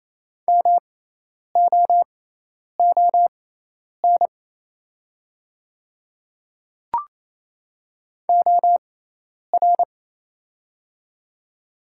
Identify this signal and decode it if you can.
Morse code, heard as a keyed tone, about 28 wpm, spaced as for 10 wpm, 700 Hz. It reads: MOON OR
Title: 6x Character Spacing - Top 500 Words - 28wpm